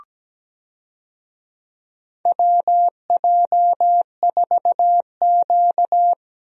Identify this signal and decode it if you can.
Morse code, heard as a keyed tone, about 17 wpm, 700 Hz. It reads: WJ4Q
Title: USA Call Signs - 4 Characters - Repeated 3x - 17wpm